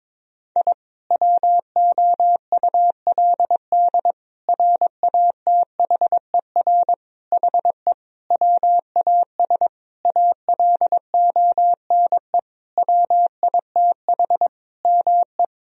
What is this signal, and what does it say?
Morse code, heard as a keyed tone, about 22 wpm, 700 Hz. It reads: I WOULD RATHER HE WAS ALONE WITH ME
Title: Sentences from Top 1000 Words - Review Part 4 - 22wpm